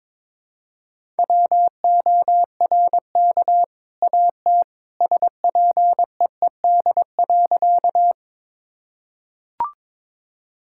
Morse code, heard as a keyed tone, about 22 wpm, 700 Hz. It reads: WORK AT SPEED.
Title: Sentences from Top 1000 Words - Review Part 5 - 22wpm